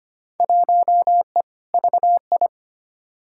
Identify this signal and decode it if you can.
Morse code, heard as a keyed tone, about 25 wpm, 700 Hz. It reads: 1E VI
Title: Field Day Exchanges - 25wpm